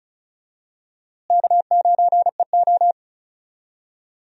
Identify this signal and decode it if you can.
Morse code, heard as a keyed tone, about 36 wpm, 700 Hz. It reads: K9EO
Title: USA Call Signs - 4 Characters - Repeated 3x - 35wpm